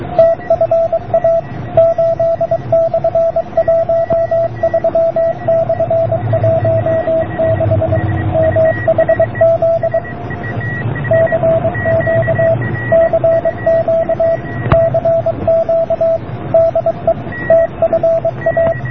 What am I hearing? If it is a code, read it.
Morse code, heard as a keyed tone, about 23 wpm, 660 Hz, with radio noise. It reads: TFA 8/13/16 MHZ CQ CQ CQ DE TFA